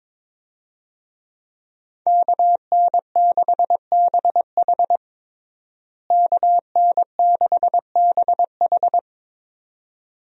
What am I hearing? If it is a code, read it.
Morse code, heard as a keyed tone, about 22 wpm, 700 Hz. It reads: KN6BH KN6BH
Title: USA Call Signs - 5 Characters - Repeated 2x - 22wpm